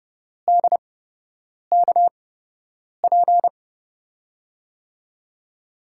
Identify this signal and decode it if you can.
Morse code, heard as a keyed tone, about 32 wpm, spaced as for 8 wpm, 700 Hz. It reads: DKP
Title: USA Call Signs - 6 Characters - 8x Character Spacing - 30wpm